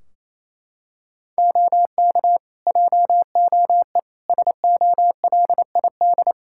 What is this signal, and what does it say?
Morse code, heard as a keyed tone, about 28 wpm, 700 Hz. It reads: OK JOE SOLID
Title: Practice QSO - Exchange 3 - More Info - 28wpm